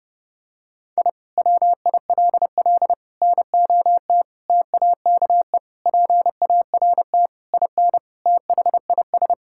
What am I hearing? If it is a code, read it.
Morse code, heard as a keyed tone, about 30 wpm, 700 Hz. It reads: I WILL NOT TAKE PART IN THIS
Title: Sentences from Top 1000 Words - Review Part 6 - 30wpm